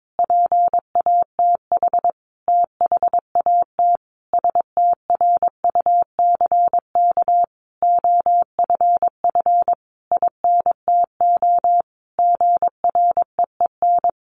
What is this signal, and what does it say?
Morse code, heard as a keyed tone, about 22 wpm, 700 Hz. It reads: PATH THAT STRUCK OFF INTO GREEN
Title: A Sound Of Thunder 22wpm